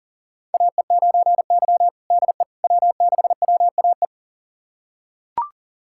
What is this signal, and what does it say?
Morse code, heard as a keyed tone, about 40 wpm, 700 Hz. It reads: AE9Y DE W6WAE
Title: USA Callsign DE Callsign 40wpm